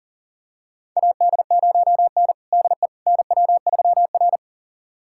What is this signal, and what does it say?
Morse code, heard as a keyed tone, about 40 wpm, 700 Hz. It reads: AD0N DE NW3R
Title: USA Callsign DE Callsign - 4 Characters - 40wpm